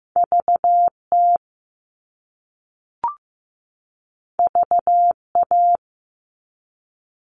Morse code, heard as a keyed tone, about 15 wpm, 700 Hz. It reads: VT VA
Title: US States 15wpm